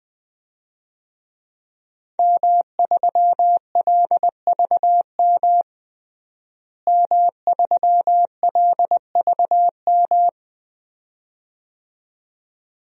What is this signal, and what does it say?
Morse code, heard as a keyed tone, about 20 wpm, 700 Hz. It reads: M3LVM M3LVM
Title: DXCC-100 Call Signs - 5 Characters - Repeated 2x - 20wpm